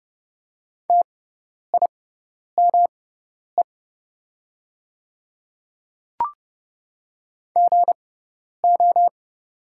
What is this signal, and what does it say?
Morse code, heard as a keyed tone, about 30 wpm, spaced as for 10 wpm, 700 Hz. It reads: TIME GO
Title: 6x Character Spacing - Top 200 Words - 30wpm